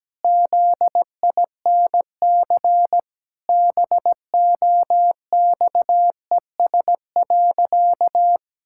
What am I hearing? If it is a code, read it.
Morse code, heard as a keyed tone, about 17 wpm, 700 Hz. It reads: ZINC BOXES.
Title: Pangram Challenge v2 - 17wpm